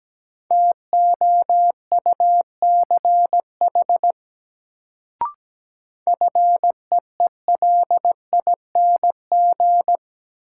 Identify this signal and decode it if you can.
Morse code, heard as a keyed tone, about 17 wpm, 700 Hz. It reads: TOUCH FEELING
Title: Sentences from Top 1200 Words - New Word Drill Alt - 17wpm